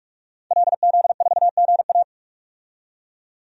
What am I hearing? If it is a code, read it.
Morse code, heard as a keyed tone, about 45 wpm, 700 Hz. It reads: LZ4GA
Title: DXCC-100 Call Signs - 5 Characters - 45wpm